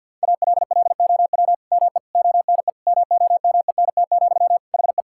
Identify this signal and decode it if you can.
Morse code, heard as a keyed tone, about 50 wpm, 700 Hz. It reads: ALLOW ME ONE MOMENT, HE
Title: Sentences from Top 1000 Words - Review Part 5 - 50wpm